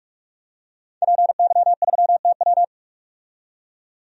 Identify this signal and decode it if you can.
Morse code, heard as a keyed tone, about 44 wpm, 700 Hz. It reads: PY3TW
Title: DXCC-100 Call Signs - 5 Characters - 45wpm